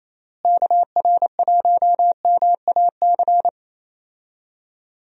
Morse code, heard as a keyed tone, about 28 wpm, 700 Hz. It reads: KR1MAC
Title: USA Call Signs - 6 Characters - Repeated 3x - 28wpm